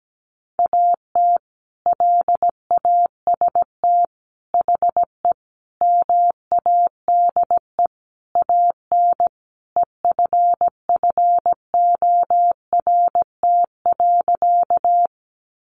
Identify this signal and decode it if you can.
Morse code, heard as a keyed tone, about 17 wpm, 700 Hz. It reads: AT LAST HE MADE AN EFFORT.
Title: Sentences from Top 1200 Words 17wpm